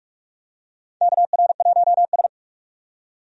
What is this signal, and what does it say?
Morse code, heard as a keyed tone, about 42 wpm, 700 Hz. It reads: KR1S
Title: USA Call Signs - 4 Characters - 45wpm